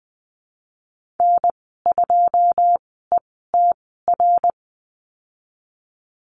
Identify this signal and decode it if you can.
Morse code, heard as a keyed tone, about 20 wpm, spaced as for 14 wpm, 700 Hz. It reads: N2ETR
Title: 082 Lesson 14g - Number 2 - Call Signs 20wpm